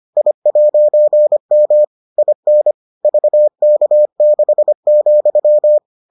Morse code, heard as a keyed tone, about 25 wpm, 600 Hz. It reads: I'M IN VK6,